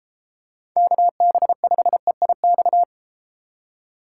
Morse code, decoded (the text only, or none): KB5EIX